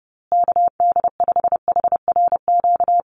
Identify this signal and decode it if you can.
Morse code, heard as a keyed tone, about 30 wpm, 700 Hz. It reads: KD5HRQ
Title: USA Call Signs Repeated 3x 30wpm